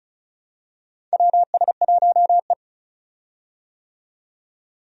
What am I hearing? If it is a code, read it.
Morse code, heard as a keyed tone, about 36 wpm, 700 Hz. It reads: WS1E
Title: USA Call Signs - 4 Characters - 35wpm